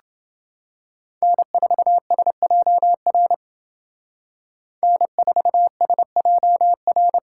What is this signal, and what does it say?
Morse code, heard as a keyed tone, about 29 wpm, 700 Hz. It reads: N4SJR N4SJR